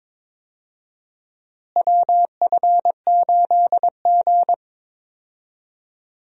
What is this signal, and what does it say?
Morse code, heard as a keyed tone, about 22 wpm, 700 Hz. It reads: WF8G
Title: USA Call Signs - 4 Characters - Repeated 3x - 22wpm